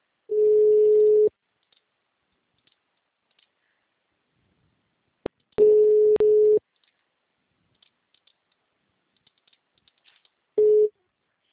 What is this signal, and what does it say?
Ringback tone (Europe, 425 Hz).
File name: out- -2006- - - .226